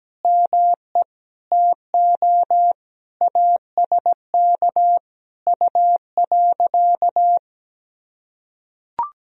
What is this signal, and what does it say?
Morse code, heard as a keyed tone, about 17 wpm, 700 Hz. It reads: ME TO ASK U.
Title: Sentences from Top 400 Words and Abbreviations 17wpm